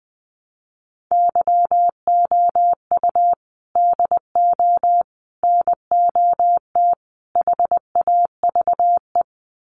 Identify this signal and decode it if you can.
Morse code, heard as a keyed tone, about 20 wpm, 700 Hz. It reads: YOU DO NOT HAVE
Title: Sets of 4 Words 20wpm